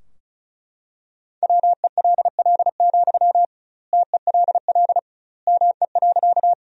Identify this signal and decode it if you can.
Morse code, heard as a keyed tone, about 36 wpm, 700 Hz. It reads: WELL, TELL ME.
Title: 2x Word-Spacing - Sentences from Top 200 Words - 35wpm